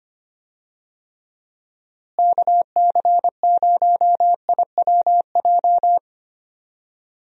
Morse code, heard as a keyed tone, about 25 wpm, 700 Hz. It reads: KC0IWJ
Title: USA Call Signs - 6 Characters - Repeated 3x - 25wpm